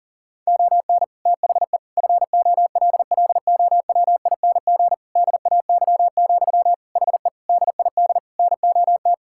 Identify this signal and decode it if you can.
Morse code, heard as a keyed tone, about 40 wpm, 700 Hz. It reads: ON THE FOLLOWING DAY, HE DID NOT
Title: Sentences from Top 1000 Words - Review Part 6 - 40wpm